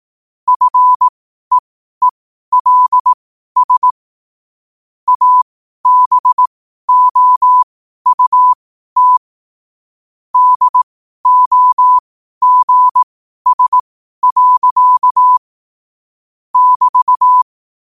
Morse code, heard as a keyed tone, about 18 wpm, spaced as for 12 wpm, 1000 Hz. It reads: FEELS ABOUT DOGS. <BT>
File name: qotdfiles / 12wpm